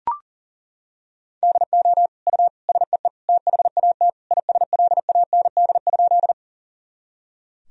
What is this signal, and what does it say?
Morse code, heard as a keyed tone, about 40 wpm, 700 Hz. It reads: DO U SEE THAT ISLAND?
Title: Sentences from Top 500 Words and Abbreviations 40wpm